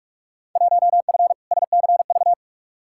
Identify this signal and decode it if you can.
Morse code, heard as a keyed tone, about 43 wpm, 700 Hz. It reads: 1F SCV
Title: Field Day Exchanges Encore - 45wpm